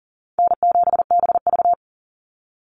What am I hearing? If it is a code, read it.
Morse code, heard as a keyed tone, about 39 wpm, 700 Hz. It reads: N7BV
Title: Easy Call Signs Repeated 3x - 40wpm